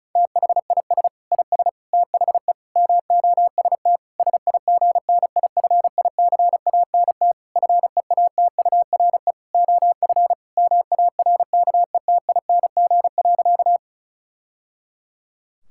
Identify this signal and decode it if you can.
Morse code, heard as a keyed tone, about 36 wpm, 700 Hz. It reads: THIS IS THE MOST SIGNIFICANT FEATURE OF MARKETING.
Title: Sentences from Top 1000 Words - Review Part 6 - 35wpm